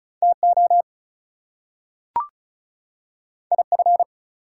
Morse code, heard as a keyed tone, about 36 wpm, 700 Hz. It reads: TO IF
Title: Two Letter Words - 35wpm